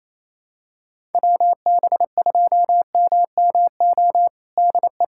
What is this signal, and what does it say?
Morse code, heard as a keyed tone, about 27 wpm, 700 Hz. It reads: WB2MMO DE
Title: USA Callsign DE Callsign - 6 Characters - 28wpm